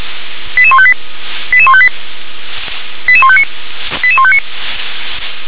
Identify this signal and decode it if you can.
Five-tone selective call (ZVEI), heard as quick runs of five tones, tones of about 70 ms. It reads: 88158 88158 88159 88159